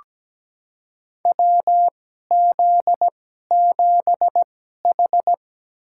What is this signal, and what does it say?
Morse code, heard as a keyed tone, about 17 wpm, spaced as for 12 wpm, 700 Hz. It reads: WZ7H
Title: USA Call Signs - 4 Characters - 2x Character Spacing - 17wpm